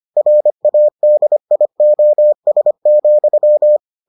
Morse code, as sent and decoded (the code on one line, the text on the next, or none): .-. .- -.. .. --- ... --..--
RADIOS,